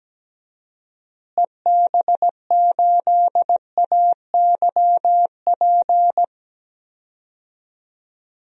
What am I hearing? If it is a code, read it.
Morse code, heard as a keyed tone, about 17 wpm, 700 Hz. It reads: EB8AYP
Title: DXCC-100 Call Signs - 6 Characters - Repeated 3x - 17wpm